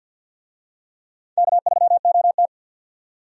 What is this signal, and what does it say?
Morse code, heard as a keyed tone, about 48 wpm, 700 Hz. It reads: K3OT
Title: USA Call Signs - 4 Characters - 50wpm